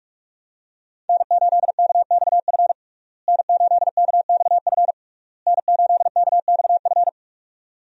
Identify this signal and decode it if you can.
Morse code, heard as a keyed tone, about 42 wpm, 700 Hz. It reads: N8KXF N8KXF N8KXF
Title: USA Call Signs - 5 Characters - Repeated 3x - 45wpm